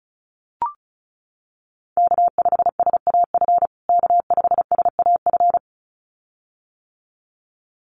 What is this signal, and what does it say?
Morse code, heard as a keyed tone, about 34 wpm, 700 Hz. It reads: K5SAF K5SAF
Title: USA Call Signs Repeated 2x 35wpm